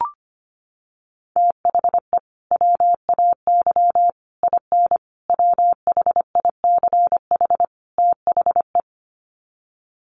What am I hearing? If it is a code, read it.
Morse code, heard as a keyed tone, about 25 wpm, 700 Hz. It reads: THE WAY IN WHICH THE
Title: Sets of 5 Words Encore2 25wpm